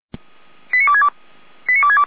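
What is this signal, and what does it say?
Five-tone selective call (ZVEI), heard as quick runs of five tones, tones of about 70 ms. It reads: 89262 89262
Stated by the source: ZVEI five-tone selective call received on an SDR.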